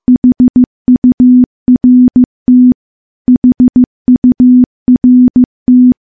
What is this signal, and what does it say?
Morse code, heard as a keyed tone, about 15 wpm, 250 Hz. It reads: HURT HURT
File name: morse-training-15WPM